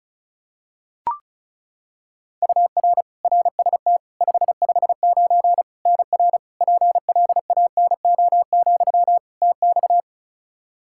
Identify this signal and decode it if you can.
Morse code, heard as a keyed tone, about 36 wpm, 700 Hz. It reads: UR RST 559 NR PLANO, TX
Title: Practice QSO - Exchange 1 - Beginning - 35wpm